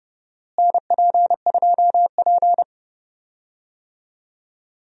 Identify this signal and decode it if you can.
Morse code, heard as a keyed tone, about 32 wpm, 700 Hz. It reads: NP2P
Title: USA Call Signs - 4 Characters - Repeated 3x - 30wpm